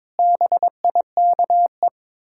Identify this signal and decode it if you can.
Morse code, heard as a keyed tone, about 22 wpm, 690 Hz. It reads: BIKE
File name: Difficult Words with Hints - 22wpm